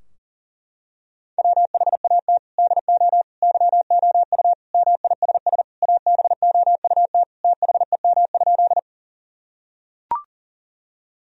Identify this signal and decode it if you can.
Morse code, heard as a keyed tone, about 40 wpm, 700 Hz. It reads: WHAT DO YOU MISS ABOUT THEM?
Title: Sentences from Top 1000 Words - Review Part 3 - 40wpm